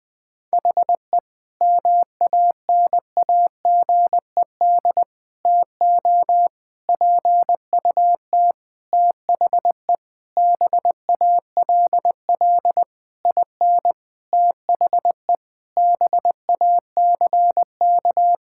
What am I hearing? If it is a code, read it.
Morse code, heard as a keyed tone, about 20 wpm, 700 Hz. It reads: HE MANAGED TO PUT THE BALL IN THE BACK
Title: Sentences with 9-Word Phrases - Spoken Beginning - 20wpm